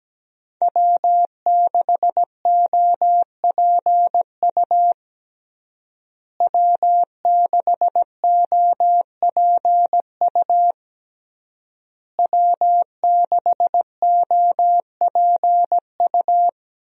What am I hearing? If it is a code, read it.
Morse code, heard as a keyed tone, about 17 wpm, 700 Hz. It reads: W6OPU W6OPU W6OPU
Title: USA Call Signs - 5 Characters - Repeated 3x - 17wpm